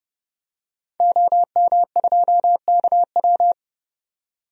Morse code, heard as a keyed tone, about 31 wpm, 700 Hz. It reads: OM2KW
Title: DXCC-100 Call Signs - 5 Characters - 30wpm